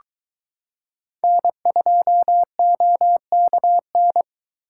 Morse code, heard as a keyed tone, about 23 wpm, 700 Hz. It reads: N2OKN